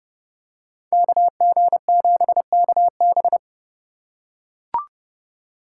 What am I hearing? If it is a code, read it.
Morse code, heard as a keyed tone, about 29 wpm, 700 Hz. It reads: KG7KB